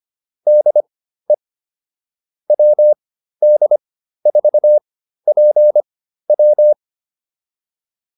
Morse code, heard as a keyed tone, about 25 wpm, spaced as for 13 wpm, 600 Hz. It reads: DE WD4PW
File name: QSO303 13